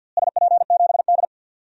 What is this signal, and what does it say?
Morse code, heard as a keyed tone, about 53 wpm, 700 Hz. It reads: SP7D
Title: DXCC-100 Call Signs - 4 Characters - Repeated 3x - 50wpm